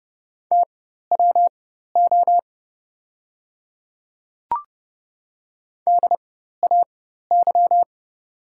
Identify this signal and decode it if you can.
Morse code, heard as a keyed tone, about 30 wpm, spaced as for 14 wpm, 700 Hz. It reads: TWO DAY
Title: 4x Character Spacing - Top 100 Words - 30wpm